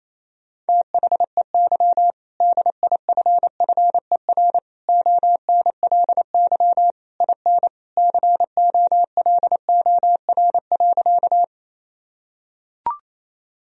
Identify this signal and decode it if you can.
Morse code, heard as a keyed tone, about 28 wpm, 700 Hz. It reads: THEY DIFFER ONLY IN COLOR.
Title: Sentences from Top 1000 Words - Review Part 2 - 28wpm